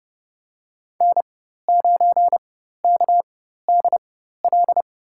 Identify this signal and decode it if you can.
Morse code, heard as a keyed tone, about 29 wpm, spaced as for 14 wpm, 700 Hz. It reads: N9KDL
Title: USA Call Signs - 5 Characters - 4x Character Spacing - 30wpm